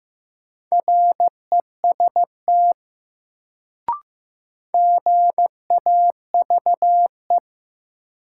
Morse code, heard as a keyed tone, about 16 wpm, 700 Hz. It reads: REST GAVE
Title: Sentences from Top 500 Words - New Word Drill - 15wpm